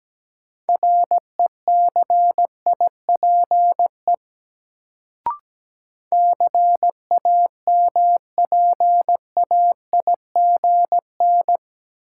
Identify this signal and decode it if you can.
Morse code, heard as a keyed tone, about 17 wpm, 700 Hz. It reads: RECIPE CAMPAIGN
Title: Sentences from Top 1200 Words - New Word Drill - 17wpm